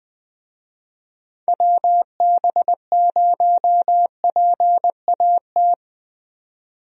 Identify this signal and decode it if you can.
Morse code, heard as a keyed tone, about 20 wpm, 700 Hz. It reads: WB0PAT
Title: USA Call Signs - 6 Characters - 20wpm